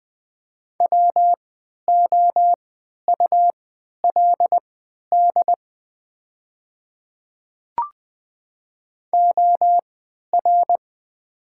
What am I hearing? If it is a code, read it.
Morse code, heard as a keyed tone, about 20 wpm, spaced as for 11 wpm, 700 Hz. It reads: WOULD OR
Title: 3x Character Spacing - Top 100 Words - 20wpm